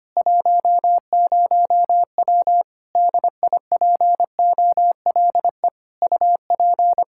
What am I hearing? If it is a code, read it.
Morse code, heard as a keyed tone, about 25 wpm, 700 Hz. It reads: 10W DIPOLE UP
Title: Practice QSO - Exchange 2 - Info - 25wpm